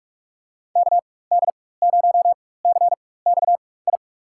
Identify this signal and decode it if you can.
Morse code, heard as a keyed tone, about 46 wpm, spaced as for 21 wpm, 700 Hz. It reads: KD0CXI
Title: USA Call Signs - 4x Character Spacing - 45wpm